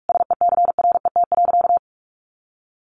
Morse code, heard as a keyed tone, about 44 wpm, 700 Hz. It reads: SECRET.
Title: Sentences from Top 1300 Words 45wpm